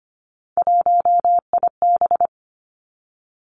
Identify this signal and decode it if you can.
Morse code, heard as a keyed tone, about 25 wpm, 700 Hz. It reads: 1IB
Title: Three Challenging Letter-Number Combinations 25wpm